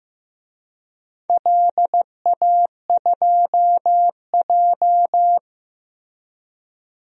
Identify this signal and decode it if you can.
Morse code, heard as a keyed tone, about 15 wpm, 700 Hz. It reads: LA2J
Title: DXCC-100 Call Signs - 4 Characters - Repeated 3x - 15wpm